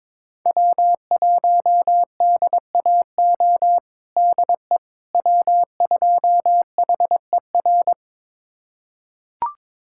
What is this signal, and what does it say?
Morse code, heard as a keyed tone, about 22 wpm, 700 Hz. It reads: W1DAO DE W2HER
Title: USA Callsign DE Callsign - 5 Characters - 22wpm